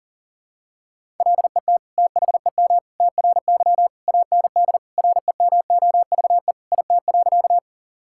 Morse code, heard as a keyed tone, about 37 wpm, 700 Hz. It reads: LET THEM TRY AND REMOVE IT.
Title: Sentences from Top 1000 Words - Review Part 5 - 40wpm